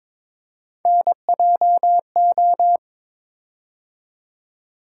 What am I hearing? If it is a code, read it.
Morse code, heard as a keyed tone, about 23 wpm, 700 Hz. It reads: NJO
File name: practice / 22wpm / Three Letter-Number Combinations - 22wpm